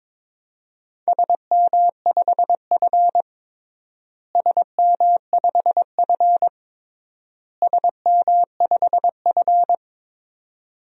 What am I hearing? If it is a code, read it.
Morse code, heard as a keyed tone, about 22 wpm, 700 Hz. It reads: SM5F SM5F SM5F
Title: DXCC-100 Call Signs - 4 Characters - Repeated 3x - 22wpm